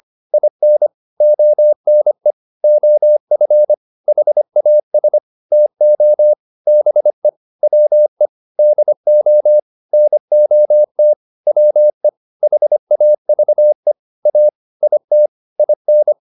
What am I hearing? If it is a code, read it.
Morse code, heard as a keyed tone, about 25 wpm, 600 Hz. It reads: IN ONE OF HAS TO BE WE DO NOT WE HAVE A IT IN